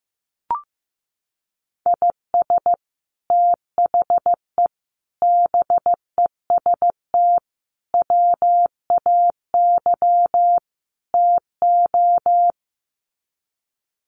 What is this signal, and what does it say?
Morse code, heard as a keyed tone, about 14 wpm, 700 Hz. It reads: IS THE BEST WAY TO
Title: Sets of 5 Words Encore 15wpm